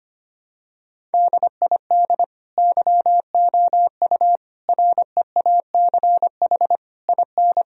Morse code, heard as a keyed tone, about 25 wpm, 700 Hz. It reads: DID YOU REACH IN
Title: Sentences from Top 1000 Words - Review Part 3 - 25wpm